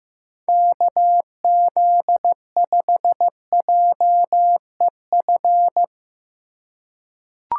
Morse code, heard as a keyed tone, about 16 wpm, 700 Hz. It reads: KZ5JEF
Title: USA Call Signs - 6 Characters - 4x Character Spacing - 15wpm